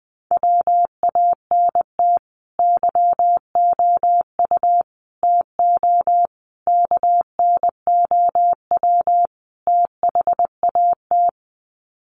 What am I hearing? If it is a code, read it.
Morse code, heard as a keyed tone, about 20 wpm, 700 Hz. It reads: WANT YOU TO KNOW THAT
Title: Sets of 5 Words Encore 20wpm